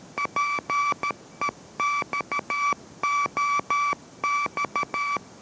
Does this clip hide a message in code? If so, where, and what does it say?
Yes: Morse code from 0.2 to 5.2 s, heard as a keyed tone, about 13 wpm, 1160 Hz. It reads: PEXOX